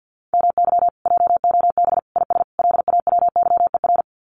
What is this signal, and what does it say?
Morse code, heard as a keyed tone, about 48 wpm, 700 Hz. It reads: MY JOB IS LAWYER
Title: Common QSO Phrases Encore 50wpm